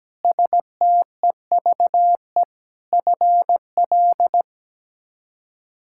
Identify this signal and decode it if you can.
Morse code, heard as a keyed tone, about 17 wpm, 700 Hz. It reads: STEVE FL
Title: K1USN SST QSO Practice - 17wpm